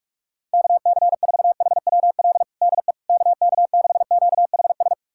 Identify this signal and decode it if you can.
Morse code, heard as a keyed tone, about 41 wpm, 700 Hz. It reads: KC4HWL DE KK6QHS
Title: USA Callsign DE Callsign 45wpm